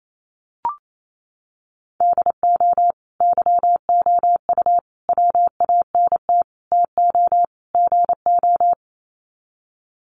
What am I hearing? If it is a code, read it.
Morse code, heard as a keyed tone, about 28 wpm, 700 Hz. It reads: DO YOU WANT TO GO
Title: Sets of 5 Words Encore2 28wpm